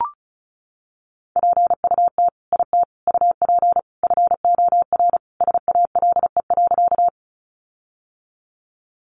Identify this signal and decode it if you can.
Morse code, heard as a keyed tone, about 34 wpm, 700 Hz. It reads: PUT IT UP FOR SALE.
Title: Sentences from Top 1400 Words 35wpm